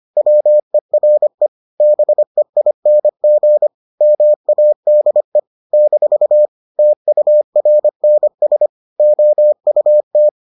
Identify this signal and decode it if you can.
Morse code, heard as a keyed tone, about 25 wpm, 600 Hz. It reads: WERE BEING MADE - TURNS OUT